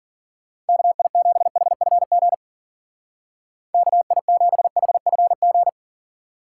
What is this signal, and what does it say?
Morse code, heard as a keyed tone, about 49 wpm, 700 Hz. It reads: KI7HFG KI7HFG
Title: Speed Racing - USA Call Signs 40wpm